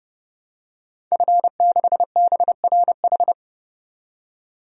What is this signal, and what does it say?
Morse code, heard as a keyed tone, about 30 wpm, 700 Hz. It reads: F6BRH
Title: DXCC-100 Call Signs - 5 Characters - 30wpm